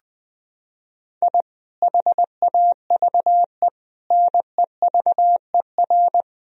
Morse code, heard as a keyed tone, about 20 wpm, 700 Hz. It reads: I HAVE NEVER
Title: Sentences from Top 1000 Words - Review Part 3 - 20wpm